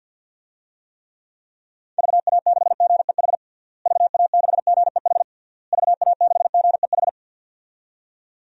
Morse code, heard as a keyed tone, about 51 wpm, 700 Hz. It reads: VA6GEH VA6GEH VA6GEH
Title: DXCC-100 Call Signs - 6 Characters - Repeated 3x - 50wpm